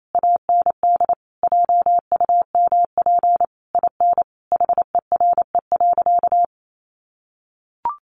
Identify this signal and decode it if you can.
Morse code, heard as a keyed tone, about 28 wpm, 700 Hz. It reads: AND JUMP IN HERE.
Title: Sentences from Top 1300 Words 28wpm